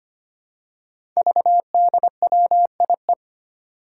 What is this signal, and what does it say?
Morse code, heard as a keyed tone, about 25 wpm, 700 Hz. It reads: VDWIE
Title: Five Challenging Letter-Number Combinations 25wpm